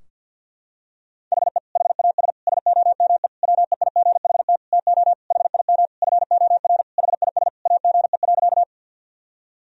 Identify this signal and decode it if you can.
Morse code, heard as a keyed tone, about 50 wpm, 700 Hz. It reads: HE HAS SOME WEIGHT TO HIM FOR HIS AGE.
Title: Sentences from Top 1000 Words - Review Part 3 - 50wpm